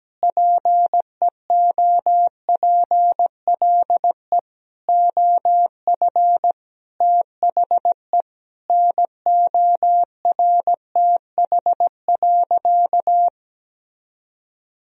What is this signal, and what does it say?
Morse code, heard as a keyed tone, about 17 wpm, 700 Hz. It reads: PEOPLE OF THE NORTH.
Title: Sentences from Top 1000 Words - Review Part 1 - 17wpm